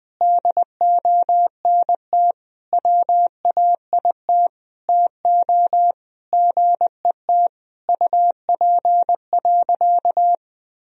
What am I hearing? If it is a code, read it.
Morse code, heard as a keyed tone, about 20 wpm, 700 Hz. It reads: DONT WAIT TO GET UP.
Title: Sentences from Top 500 Words and Abbreviations 20wpm